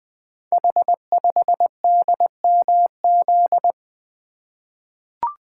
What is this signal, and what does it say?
Morse code, heard as a keyed tone, about 20 wpm, 700 Hz. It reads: H5DMZ
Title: Code Groups - 5 Letter:Numbers - 20wpm